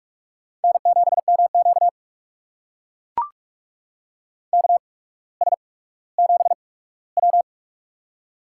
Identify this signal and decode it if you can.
Morse code, heard as a keyed tone, about 44 wpm, spaced as for 19 wpm, 700 Hz. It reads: N7MQ KS7W
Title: USA Call Signs - 4 Characters - 8x Character Spacing - 45wpm